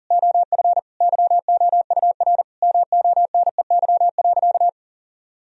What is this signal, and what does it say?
Morse code, heard as a keyed tone, about 40 wpm, 700 Hz. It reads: OF YOUR MONEY.
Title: Sentences from Top 1000 Words - Review Part 2 - 40wpm